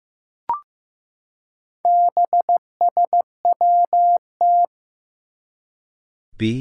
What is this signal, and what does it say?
Morse code, heard as a keyed tone, about 15 wpm, 700 Hz. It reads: BSWT
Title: Four Challenging Letter-Number Combinations 15wpm